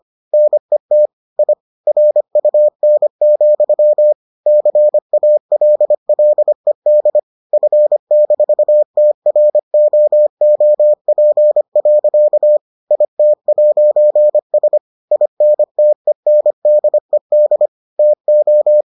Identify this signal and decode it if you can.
Morse code, heard as a keyed tone, about 25 wpm, 600 Hz. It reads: NET I RUN, CALLED F-TROOP. IT'S INTENDED TO